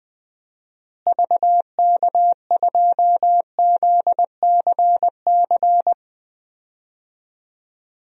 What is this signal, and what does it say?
Morse code, heard as a keyed tone, about 20 wpm, 700 Hz. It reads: VK2ZCC
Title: DXCC-100 Call Signs - 6 Characters - Repeated 2x - 20wpm